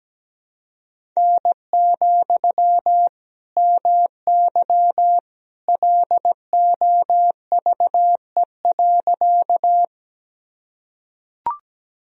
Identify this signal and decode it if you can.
Morse code, heard as a keyed tone, about 17 wpm, 700 Hz. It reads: N, MY LOVE.
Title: Sentences from Top 400 Words and Abbreviations 17wpm